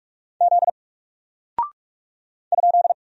Morse code, heard as a keyed tone, about 46 wpm, 700 Hz. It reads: Z ?
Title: Single Character (Letter:Number:Punctuation:Prosign) Alt 45wpm